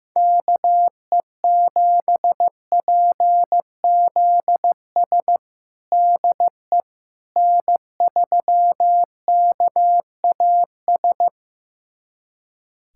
Morse code, heard as a keyed tone, about 15 wpm, 700 Hz. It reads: KE7PZS DE N3KAS
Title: USA Callsign DE Callsign 15wpm